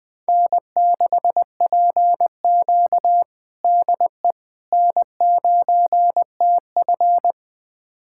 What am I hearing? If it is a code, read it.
Morse code, heard as a keyed tone, about 20 wpm, 700 Hz. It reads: N6PQ DE N9TF
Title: USA Callsign DE Callsign - 4 Characters - 20wpm